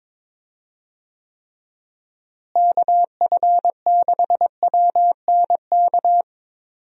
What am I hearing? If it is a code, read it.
Morse code, heard as a keyed tone, about 22 wpm, 700 Hz. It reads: KF6WNK
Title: USA Call Signs - 6 Characters - Repeated 3x - 22wpm